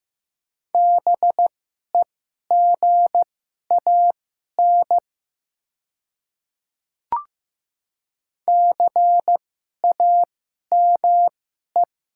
Morse code, heard as a keyed tone, about 15 wpm, spaced as for 11 wpm, 700 Hz. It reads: BEGAN CAME
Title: 2x Character Spacing - Top 300 Words - 15wpm